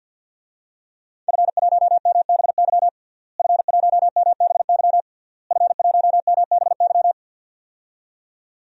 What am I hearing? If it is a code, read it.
Morse code, heard as a keyed tone, about 52 wpm, 700 Hz. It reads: F1MBY F1MBY F1MBY
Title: DXCC-100 Call Signs - 5 Characters - Repeated 3x - 50wpm